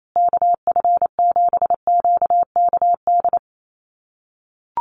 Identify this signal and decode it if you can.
Morse code, heard as a keyed tone, about 28 wpm, 700 Hz. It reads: KF7QKD
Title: USA Call Signs Repeated 2x 28wpm